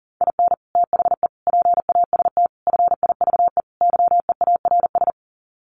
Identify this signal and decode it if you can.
Morse code, heard as a keyed tone, about 37 wpm, 700 Hz. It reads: IN THE PAST FIVE YEARS
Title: Sets of 5 Words Encore 40wpm